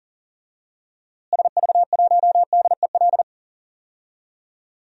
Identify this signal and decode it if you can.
Morse code, heard as a keyed tone, about 39 wpm, 700 Hz. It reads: SV1DEL
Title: DXCC-100 Call Signs - 6 Characters - Repeated 2x - 40wpm